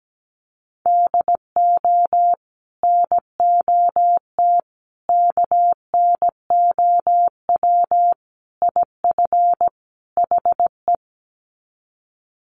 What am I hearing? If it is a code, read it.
Morse code, heard as a keyed tone, about 17 wpm, 700 Hz. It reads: DO NOT KNOW IF HE
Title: Sets of 5 Words Encore2 17wpm